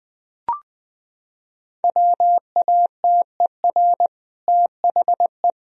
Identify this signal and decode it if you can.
Morse code, heard as a keyed tone, about 20 wpm, 700 Hz. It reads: WATER THE
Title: Sentences from Top 1000 Words - Review Part 1 - 20wpm